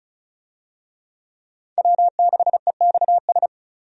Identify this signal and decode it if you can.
Morse code, heard as a keyed tone, about 34 wpm, 700 Hz. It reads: W6EXS
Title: USA Call Signs - 5 Characters - Repeated 3x - 35wpm